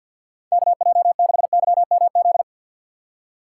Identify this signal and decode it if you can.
Morse code, heard as a keyed tone, about 53 wpm, 700 Hz. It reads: KJ6YMZ
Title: USA Call Signs - 3x Character Spacing - 50wpm